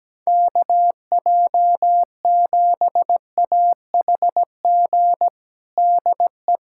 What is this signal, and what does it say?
Morse code, heard as a keyed tone, about 17 wpm, 700 Hz. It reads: KJ7AHG DE
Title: USA Callsign DE Callsign 17wpm